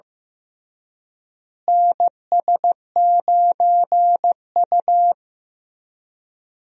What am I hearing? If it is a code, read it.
Morse code, heard as a keyed tone, about 15 wpm, 700 Hz. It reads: NS9U
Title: USA Call Signs - 4 Characters - Repeated 2x - 15wpm